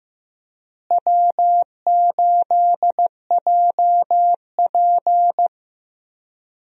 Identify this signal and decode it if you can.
Morse code, heard as a keyed tone, about 15 wpm, 700 Hz. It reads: W8JP
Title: USA Call Signs - 4 Characters - Repeated 3x - 15wpm